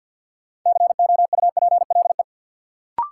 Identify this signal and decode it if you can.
Morse code, heard as a keyed tone, about 50 wpm, 700 Hz. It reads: COUPLE
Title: Sentences from Top 1100 Words - New Word Drill Alt - 50wpm